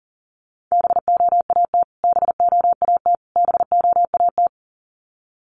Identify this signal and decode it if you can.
Morse code, heard as a keyed tone, about 40 wpm, 700 Hz. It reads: BOAT BOAT BOAT
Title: Top 500 Words Repeated 40wpm